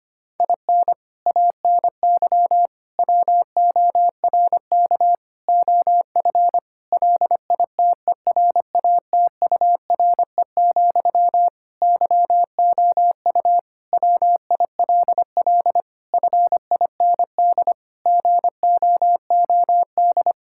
Morse code, heard as a keyed tone, about 25 wpm, 700 Hz. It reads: IN ANY WORK OF LITERATURE, YOU WILL FIND GOOD
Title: Sentences from Top 1000 Words - Review Part 6 - 25wpm